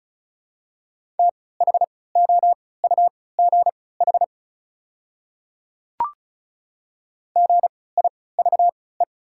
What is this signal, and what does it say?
Morse code, heard as a keyed tone, about 36 wpm, spaced as for 20 wpm, 700 Hz. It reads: THOUGH GIVE
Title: 3x Character Spacing - Top 500 Words - 35wpm